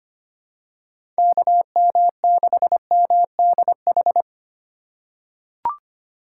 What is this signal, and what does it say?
Morse code, heard as a keyed tone, about 25 wpm, 700 Hz. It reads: KM6MDH
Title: USA Call Signs - 2x Character Spacing - 25wpm